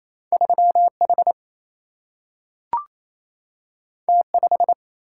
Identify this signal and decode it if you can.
Morse code, heard as a keyed tone, about 28 wpm, 700 Hz. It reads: 3H T5
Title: Single-Letter Single-Number 28wpm